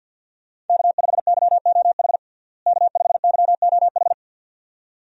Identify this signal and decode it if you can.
Morse code, heard as a keyed tone, about 52 wpm, 700 Hz. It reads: K5YOH K5YOH
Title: USA Call Signs - 5 Characters - Repeated 2x - 50wpm